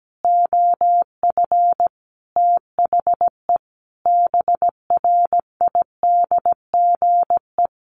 Morse code, heard as a keyed tone, about 17 wpm, 700 Hz. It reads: OF THE BRIDGE
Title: Sentences from Top 1400 Words 17wpm